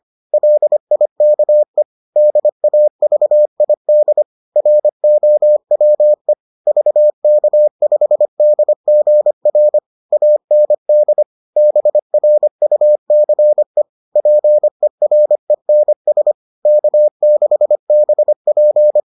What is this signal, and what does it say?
Morse code, heard as a keyed tone, about 25 wpm, 600 Hz. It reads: LIKE DAVID ROWE VK5DGR AND BRUCE PERENS K6BP